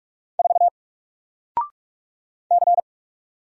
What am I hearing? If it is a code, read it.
Morse code, heard as a keyed tone, about 44 wpm, 700 Hz. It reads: 4 C
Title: Single Character (Letter:Number:Punctuation:Prosign) 45wpm